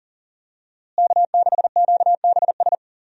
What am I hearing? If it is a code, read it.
Morse code, heard as a keyed tone, about 40 wpm, 700 Hz. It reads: K6QBS
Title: USA Call Signs - 5 Characters - Repeated 2x - 40wpm